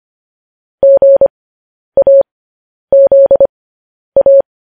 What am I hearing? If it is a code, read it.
Morse code, heard as a keyed tone, about 25 wpm, spaced as for 10 wpm, 550 Hz. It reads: GAZA